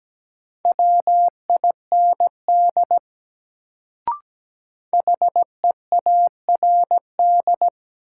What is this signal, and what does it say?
Morse code, heard as a keyed tone, about 17 wpm, 700 Hz. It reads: WIND HEARD
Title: Sentences from Top 400 Words - New Word Drill - 17wpm